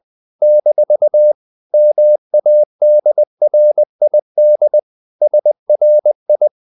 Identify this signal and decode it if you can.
Morse code, heard as a keyed tone, about 20 wpm, 600 Hz. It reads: - MADRID SRI
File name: CW / Practice / NationalCapitals20